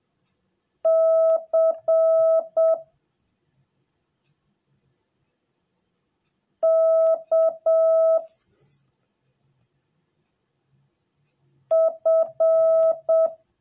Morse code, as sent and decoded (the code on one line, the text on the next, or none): -.-. / -.- / ..-.
C K F